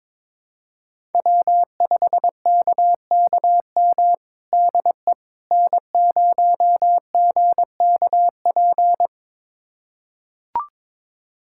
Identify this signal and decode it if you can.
Morse code, heard as a keyed tone, about 22 wpm, 700 Hz. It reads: W5KKM DE N0GKP
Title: USA Callsign DE Callsign - 5 Characters - 22wpm